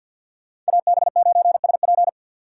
Morse code, heard as a keyed tone, about 64 wpm, 700 Hz. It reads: AB9SP